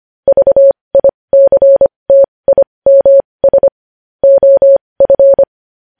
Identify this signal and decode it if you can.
Morse code, heard as a keyed tone, about 25 wpm, spaced as for 20 wpm, 550 Hz. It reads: VICTIMS OF
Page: News headlines in Morse code at 20 wpm